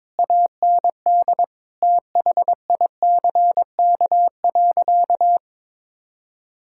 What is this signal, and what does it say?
Morse code, heard as a keyed tone, about 22 wpm, 700 Hz. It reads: AND THICK.
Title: Sentences from Top 1600 Words 22wpm